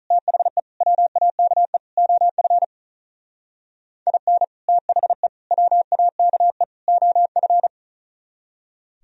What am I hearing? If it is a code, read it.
Morse code, heard as a keyed tone, about 43 wpm, 700 Hz. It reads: THE WAKE OF IN THE WAKE OF
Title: Speed-Racing - Sets of 4 Words 35wpm